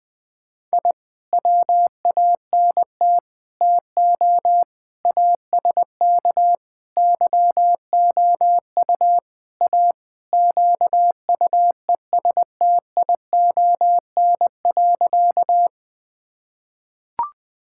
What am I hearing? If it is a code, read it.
Morse code, heard as a keyed tone, about 20 wpm, 700 Hz. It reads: I WANT TO ASK YOU A QUESTION.
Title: Sentences with 5-Word Phrases - 20wpm